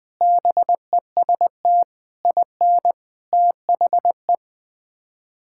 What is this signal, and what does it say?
Morse code, heard as a keyed tone, about 20 wpm, 700 Hz. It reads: BEST IN THE
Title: Sets of 6 Words - 20wpm